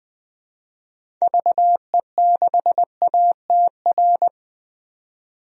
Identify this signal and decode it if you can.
Morse code, heard as a keyed tone, about 21 wpm, 700 Hz. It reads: VE6ATR
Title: DXCC-100 Call Signs - 6 Characters - 20wpm